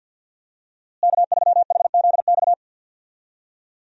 Morse code, heard as a keyed tone, about 53 wpm, 700 Hz. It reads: K3HZX
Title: USA Call Signs - 5 Characters - Repeated 2x - 50wpm